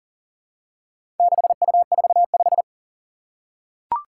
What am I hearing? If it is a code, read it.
Morse code, heard as a keyed tone, about 37 wpm, 700 Hz. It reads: 6U45
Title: Four Challenging Letter-Number Combinations 40wpm